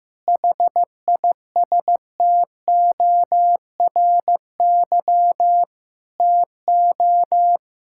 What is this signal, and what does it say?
Morse code, heard as a keyed tone, about 15 wpm, 700 Hz. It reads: HISTORY TO
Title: Sentences from Top 1000 Words - Review Part 6 - 15wpm